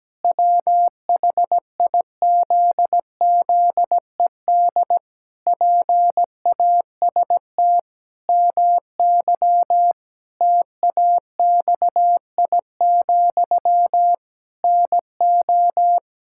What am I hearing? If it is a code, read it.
Morse code, heard as a keyed tone, about 17 wpm, 700 Hz. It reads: WHIZZED PAST MY TAXI, NO
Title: Pangram Challenge v2 - 17wpm